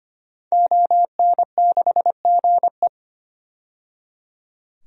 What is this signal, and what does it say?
Morse code, heard as a keyed tone, about 25 wpm, 700 Hz. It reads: ON6GE
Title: DXCC-100 Call Signs - 5 Characters - 25wpm